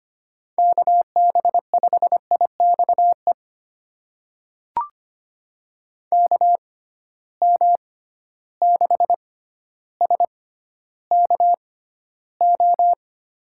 Morse code, heard as a keyed tone, about 25 wpm, spaced as for 13 wpm, 700 Hz. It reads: KB5IXE KM6SKO
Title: USA Call Signs - 6 Characters - 6x Character Spacing - 25wpm